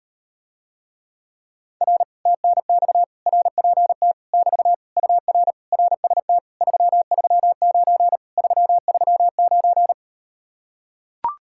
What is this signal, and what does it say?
Morse code, heard as a keyed tone, about 38 wpm, 700 Hz. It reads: R TNX RPT <BT> UR RST 339 339